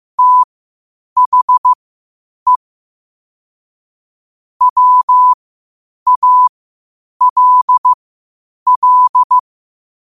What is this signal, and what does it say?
Morse code, heard as a keyed tone, about 15 wpm, spaced as for 8 wpm, 1000 Hz. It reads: THE WALL
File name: qotdfiles / 8wpm